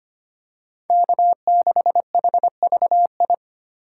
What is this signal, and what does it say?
Morse code, heard as a keyed tone, about 25 wpm, 700 Hz. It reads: K6HVI
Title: USA Call Signs - 5 Characters - 2x Character Spacing - 25wpm